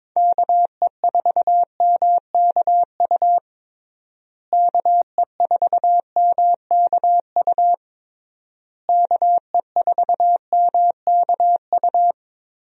Morse code, heard as a keyed tone, about 21 wpm, 700 Hz. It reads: KE4MKU KE4MKU KE4MKU
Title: USA Call Signs - 6 Characters - Repeated 3x - 22wpm